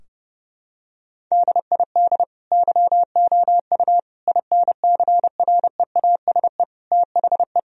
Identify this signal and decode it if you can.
Morse code, heard as a keyed tone, about 30 wpm, 700 Hz. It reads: DID YOU INCREASE THE
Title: Sentences from Top 1000 Words - Review Part 4 - 30wpm